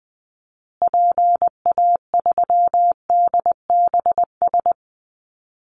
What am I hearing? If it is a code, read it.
Morse code, heard as a keyed tone, about 20 wpm, 700 Hz. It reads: PA3DBS
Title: Call Signs Repeated 3x - 20wpm